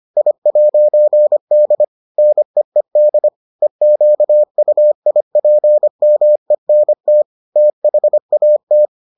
Morse code, heard as a keyed tone, about 25 wpm, 600 Hz. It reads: I'D NEED EQUIPMENT THAT